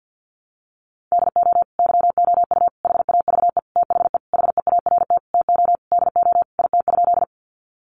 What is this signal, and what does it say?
Morse code, heard as a keyed tone, about 49 wpm, 700 Hz. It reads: DO YOU HAVE THE HEART TO DO IT?
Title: Sentences from Top 1200 Words 50wpm